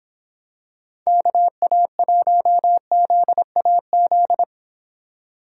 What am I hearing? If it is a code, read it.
Morse code, heard as a keyed tone, about 26 wpm, 700 Hz. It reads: KA1ZAZ